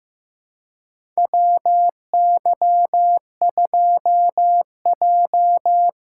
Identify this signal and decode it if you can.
Morse code, heard as a keyed tone, about 15 wpm, 700 Hz. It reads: WY2J
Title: USA Callsign DE Callsign - 4 Characters - 15wpm